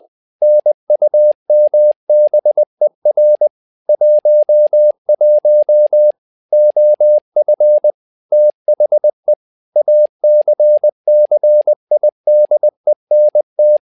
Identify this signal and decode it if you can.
Morse code, heard as a keyed tone, about 20 wpm, 600 Hz. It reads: NUMBER 11 OF THE ACCIDENT
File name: tower trouble 20wpm